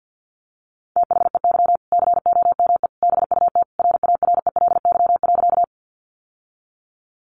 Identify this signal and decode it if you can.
Morse code, heard as a keyed tone, about 50 wpm, 700 Hz. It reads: THEY COME BUT RARELY.
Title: Sentences from Top 1400 Words 50wpm